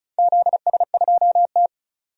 Morse code, heard as a keyed tone, about 36 wpm, 700 Hz. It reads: ZS2T
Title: DXCC-100 Call Signs - 4 Characters - 35wpm